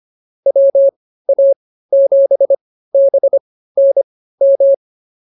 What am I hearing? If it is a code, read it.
Morse code, heard as a keyed tone, about 24 wpm, spaced as for 15 wpm, 550 Hz. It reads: WA7BNM
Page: intermediate-practice-files